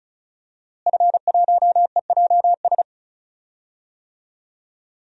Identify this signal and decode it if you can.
Morse code, heard as a keyed tone, about 36 wpm, 700 Hz. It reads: F1EJS
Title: DXCC-100 Call Signs - 5 Characters - Repeated 3x - 35wpm